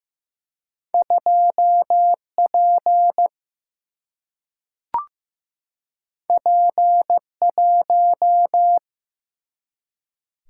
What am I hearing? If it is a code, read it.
Morse code, heard as a keyed tone, about 15 wpm, 700 Hz. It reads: 2 P P 1
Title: Single-Letter Single-Number 15wpm